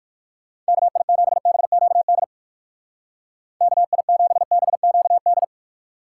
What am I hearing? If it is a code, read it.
Morse code, heard as a keyed tone, about 55 wpm, 700 Hz. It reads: KI7BQD KI7BQD
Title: Speed Racing - USA 2x3 Call Signs 45wpm